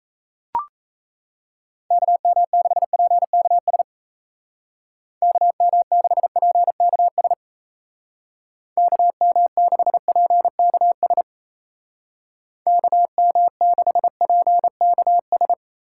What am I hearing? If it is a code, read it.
Morse code, heard as a keyed tone, about 34 wpm, 700 Hz. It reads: KM6PKS KM6PKS KM6PKS KM6PKS
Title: Speed Racing - USA Call Signs 28wpm